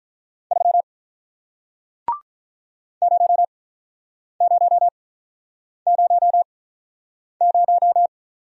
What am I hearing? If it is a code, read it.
Morse code, heard as a keyed tone, about 47 wpm, spaced as for 9 wpm, 700 Hz. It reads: 3 0000